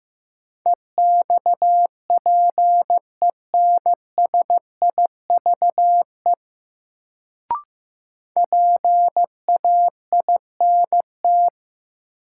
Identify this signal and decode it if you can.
Morse code, heard as a keyed tone, about 15 wpm, 700 Hz. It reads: EXPENSIVE PAINT
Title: Sentences from Top 1000 Words - Review Part 6 - New Word Drill - 15wpm